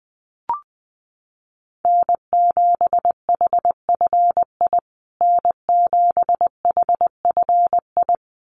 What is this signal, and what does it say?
Morse code, heard as a keyed tone, about 20 wpm, 700 Hz. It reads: N7HFI N7HFI
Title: USA Call Signs Repeated 2x 20wpm